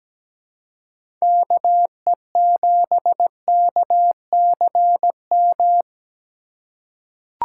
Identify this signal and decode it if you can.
Morse code, heard as a keyed tone, about 17 wpm, 700 Hz. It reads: KE7KCM
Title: USA Call Signs - 3x Character Spacing - 17wpm